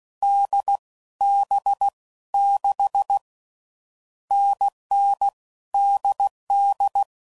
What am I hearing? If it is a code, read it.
Morse code, heard as a keyed tone, about 16 wpm, 800 Hz. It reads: D B 6 NN DD
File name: Combinations / Difficult Combos / NDB NDB6